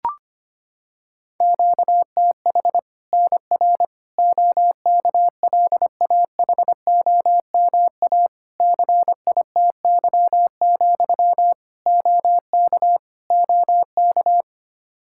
Morse code, heard as a keyed tone, about 25 wpm, 700 Hz. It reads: QTH NR OKLAHOMA CITY, OK OK
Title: Practice QSO - Exchange 1 - Beginning - 25wpm